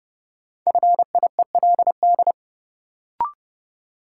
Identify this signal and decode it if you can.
Morse code, heard as a keyed tone, about 30 wpm, 700 Hz. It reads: FIELD
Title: Sentences from Top 1000 Words - New Word Drill - 30wpm